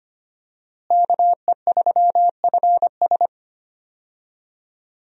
Morse code, heard as a keyed tone, about 25 wpm, 700 Hz. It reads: KE3FS
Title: USA Call Signs - 5 Characters - Repeated 2x - 25wpm